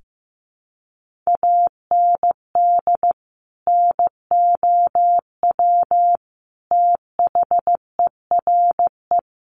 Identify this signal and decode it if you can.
Morse code, heard as a keyed tone, about 15 wpm, 700 Hz. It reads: AND NOW THERE
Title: Sentences from Top 1400 Words 15wpm